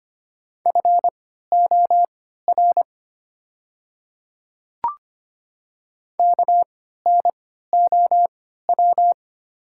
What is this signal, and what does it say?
Morse code, heard as a keyed tone, about 25 wpm, spaced as for 14 wpm, 700 Hz. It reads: FOR KNOW
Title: 3x Character Spacing - Top 100 Words - 25wpm